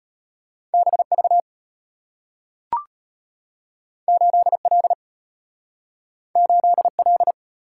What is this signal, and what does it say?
Morse code, heard as a keyed tone, about 34 wpm, 700 Hz. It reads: BV 8L 8L